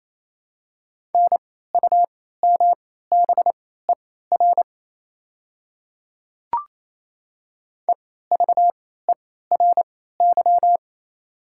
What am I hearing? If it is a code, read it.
Morse code, heard as a keyed tone, about 28 wpm, spaced as for 16 wpm, 700 Hz. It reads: NUMBER EVERY
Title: 3x Character Spacing - Top 200 Words - 28wpm